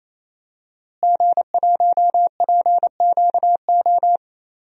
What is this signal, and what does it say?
Morse code, heard as a keyed tone, about 28 wpm, 700 Hz. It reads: G1PQO
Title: DXCC-100 Call Signs - 5 Characters - Repeated 3x - 28wpm